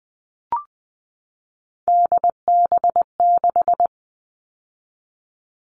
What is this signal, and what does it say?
Morse code, heard as a keyed tone, about 20 wpm, 700 Hz. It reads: DB6
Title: All Cut Numbers - 3-Digit 20wpm